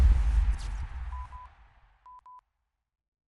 There is no code.